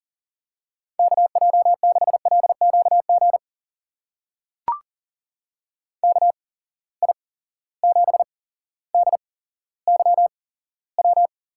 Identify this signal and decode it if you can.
Morse code, heard as a keyed tone, about 39 wpm, spaced as for 17 wpm, 700 Hz. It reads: KJ6LQG KI7DYW
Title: USA Call Signs - 6 Characters - 8x Character Spacing - 40wpm